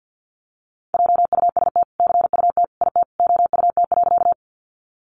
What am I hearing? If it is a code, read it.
Morse code, heard as a keyed tone, about 50 wpm, 700 Hz. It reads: JUST CUT IT OUT.
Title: Sentences from Top 300 Words 50wpm